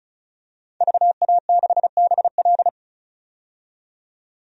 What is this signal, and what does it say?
Morse code, heard as a keyed tone, about 36 wpm, 700 Hz. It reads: VA6BL
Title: DXCC-100 Call Signs - 5 Characters - 35wpm